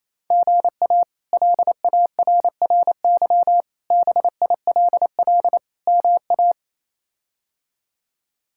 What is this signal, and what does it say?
Morse code, heard as a keyed tone, about 27 wpm, 700 Hz. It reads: GA LARRY BILL MA
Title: K1USN SST QSO Practice - 28wpm